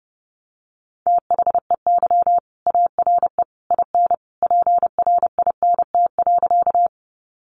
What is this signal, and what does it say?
Morse code, heard as a keyed tone, about 30 wpm, 700 Hz. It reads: THEY ARE IN PRINT.
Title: Sentences from Top 1400 Words 30wpm